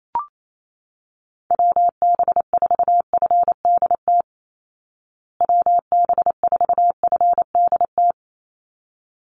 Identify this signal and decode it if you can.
Morse code, heard as a keyed tone, about 28 wpm, 700 Hz. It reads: WB4FDT WB4FDT
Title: Call Signs Repeated 3x - 28wpm